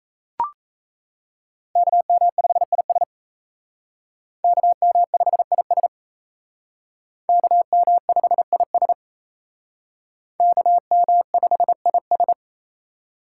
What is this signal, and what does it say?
Morse code, heard as a keyed tone, about 36 wpm, 700 Hz. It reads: KM5IS KM5IS KM5IS KM5IS